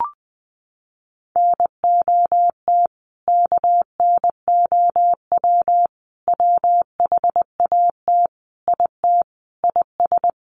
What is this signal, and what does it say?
Morse code, heard as a keyed tone, about 20 wpm, 700 Hz. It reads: NOT KNOW WHAT IT IS
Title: Sets of 5 Words Encore 20wpm